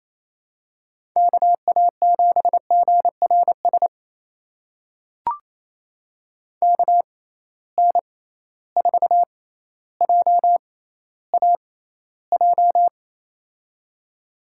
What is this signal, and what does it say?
Morse code, heard as a keyed tone, about 28 wpm, spaced as for 14 wpm, 700 Hz. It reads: KA7GRS KN4JAJ
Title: USA Call Signs - 6x Character Spacing - 28wpm